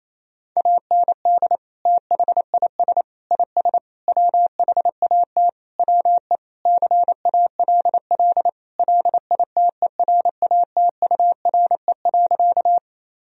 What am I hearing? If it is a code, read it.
Morse code, heard as a keyed tone, about 27 wpm, 700 Hz. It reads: AND THIS IS WHAT WE CALL LITERATURE.
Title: Sentences from Top 1000 Words - Review Part 3 - 28wpm